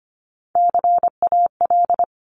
Morse code, heard as a keyed tone, about 25 wpm, 700 Hz. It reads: CAL
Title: Top Three Letter Combinations in Words 25wpm